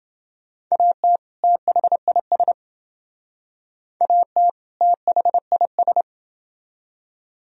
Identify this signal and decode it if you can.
Morse code, heard as a keyed tone, about 28 wpm, 700 Hz. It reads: AT THIS AT THIS